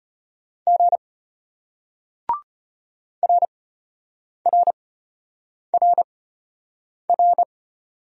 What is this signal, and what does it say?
Morse code, heard as a keyed tone, about 31 wpm, spaced as for 8 wpm, 700 Hz. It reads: G RRRR
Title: Speed-Racing Single Letters 25wpm